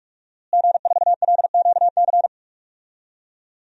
Morse code, heard as a keyed tone, about 42 wpm, 700 Hz. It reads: G4LQC
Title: DXCC-100 Call Signs - 5 Characters - Repeated 2x - 45wpm